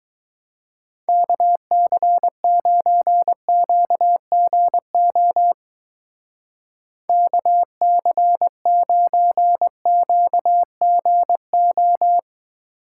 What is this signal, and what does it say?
Morse code, heard as a keyed tone, about 20 wpm, 700 Hz. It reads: KC9QGO KC9QGO